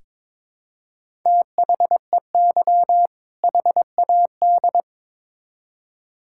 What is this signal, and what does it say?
Morse code, heard as a keyed tone, about 21 wpm, 700 Hz. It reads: THEY HAD
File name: Build A Phrase - 2-Word Phrases - 22wpm